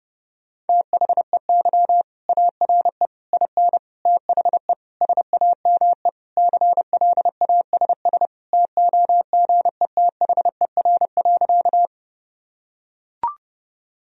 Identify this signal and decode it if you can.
Morse code, heard as a keyed tone, about 30 wpm, 700 Hz. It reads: THEY ARE IN THE SAME CLASS TOGETHER.
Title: Sentences from Top 1000 Words - Review Part 2 - 30wpm